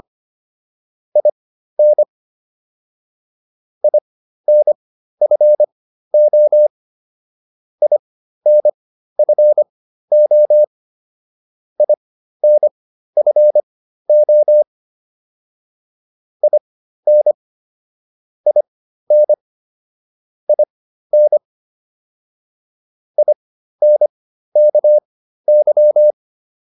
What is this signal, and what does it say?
Morse code, heard as a keyed tone, about 25 wpm, spaced as for 13 wpm, 600 Hz. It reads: IN INFO INFO INFO IN IN IN INKY